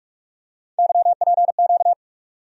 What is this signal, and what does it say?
Morse code, heard as a keyed tone, about 44 wpm, 700 Hz. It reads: YPQ
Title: USA Call Sign Postfixes - 3 Letters 45wpm